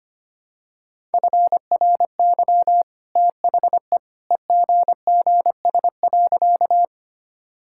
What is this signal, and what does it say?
Morse code, heard as a keyed tone, about 25 wpm, 700 Hz. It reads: FRY THE EGGS.
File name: Easy Sentences with Hints - 25wpm